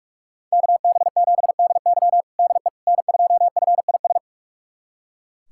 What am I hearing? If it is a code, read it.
Morse code, heard as a keyed tone, about 41 wpm, 700 Hz. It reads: KB7DY DE N2FIS
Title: USA Callsign DE Callsign - 5 Characters - 45wpm